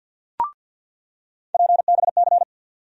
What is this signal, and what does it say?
Morse code, heard as a keyed tone, about 47 wpm, 700 Hz. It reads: PBC
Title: USA Call Sign Postfixes - 3 Letters 50wpm